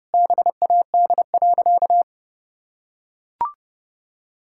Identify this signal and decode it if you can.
Morse code, heard as a keyed tone, about 30 wpm, 700 Hz. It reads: BAD.
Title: Sentences from Top 1000 Words - Review Part 5 - 30wpm